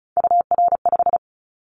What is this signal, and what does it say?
Morse code, heard as a keyed tone, about 34 wpm, 700 Hz. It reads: UR5
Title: Callsign Prefixes for Top 25 DXCC - Repeated 3x - 35wpm